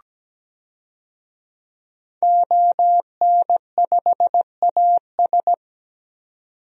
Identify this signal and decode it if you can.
Morse code, heard as a keyed tone, about 17 wpm, 700 Hz. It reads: ON5AS
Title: DXCC-100 Call Signs - 5 Characters - 17wpm